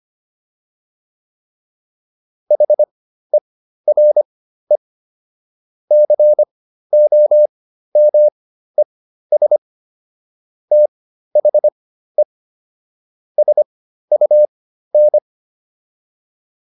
Morse code, heard as a keyed tone, about 25 wpm, spaced as for 13 wpm, 600 Hz. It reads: HERE COMES THE SUN